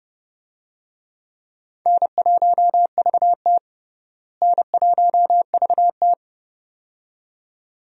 Morse code, heard as a keyed tone, about 32 wpm, 700 Hz. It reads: N1VT N1VT
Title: USA Call Signs - 4 Characters - Repeated 2x - 30wpm